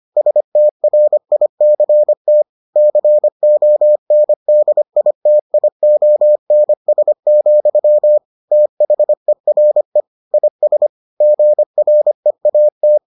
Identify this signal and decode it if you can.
Morse code, heard as a keyed tone, about 25 wpm, 600 Hz. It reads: STRICT CONDITIONS, THERE IS GREAT